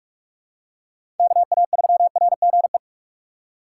Morse code, heard as a keyed tone, about 46 wpm, 700 Hz. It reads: KA3RGE
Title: USA Call Signs - 6 Characters - 6x Character Spacing - 45wpm